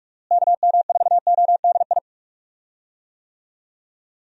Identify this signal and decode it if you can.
Morse code, heard as a keyed tone, about 46 wpm, 700 Hz. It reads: KM4ODI
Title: USA Call Signs - 6 Characters - Repeated 2x - 45wpm